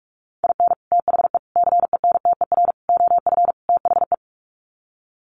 Morse code, heard as a keyed tone, about 44 wpm, 700 Hz. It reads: IN THE CENTER OF THE
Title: Sets of 5 Words 45wpm